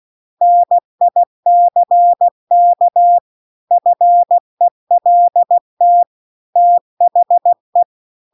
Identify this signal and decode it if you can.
Morse code, heard as a keyed tone, about 16 wpm, 700 Hz. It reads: NICK FELT THE
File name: CW-The-Big-Twoheaded-River-2-16WPM